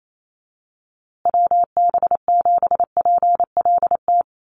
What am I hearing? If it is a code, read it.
Morse code, heard as a keyed tone, about 28 wpm, 700 Hz. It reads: WB7PLT
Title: USA Call Signs Repeated 3x 28wpm